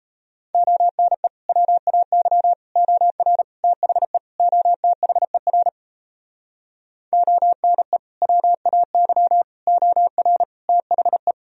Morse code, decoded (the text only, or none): ONE WAY OR THE OTHER ONE WAY OR THE